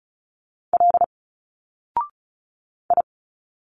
Morse code, heard as a keyed tone, about 35 wpm, 700 Hz. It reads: L I
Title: Single Letters 35wpm